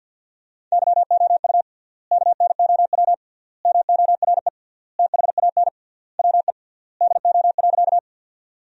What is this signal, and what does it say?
Morse code, heard as a keyed tone, about 50 wpm, 700 Hz. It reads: YOU KNOW MORE THAN WE DO.
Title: 3x Word-Spacing - Sentences from Top 100 Words - 50wpm